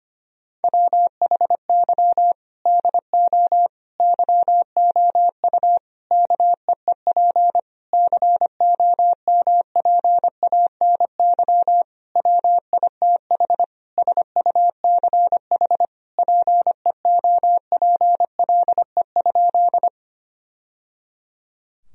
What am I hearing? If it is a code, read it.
Morse code, heard as a keyed tone, about 25 wpm, 700 Hz. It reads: WHY DO YOU KEEP COMPANY WITH SUCH PEOPLE?
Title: Sentences from Top 1000 Words - Review Part 3 - 25wpm